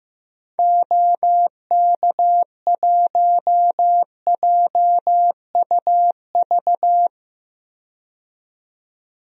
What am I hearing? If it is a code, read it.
Morse code, heard as a keyed tone, about 15 wpm, 700 Hz. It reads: OK1JUV
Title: DXCC-100 Call Signs - 6 Characters - 15wpm